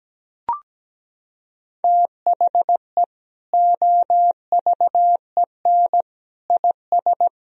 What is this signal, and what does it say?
Morse code, heard as a keyed tone, about 17 wpm, 700 Hz. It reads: THE OVEN IS
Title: Sentences from Top 1000 Words - Review Part 4 - 17wpm